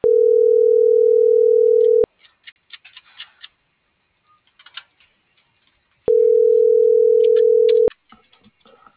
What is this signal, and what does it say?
Ringback tone (North America), 440 + 480 Hz.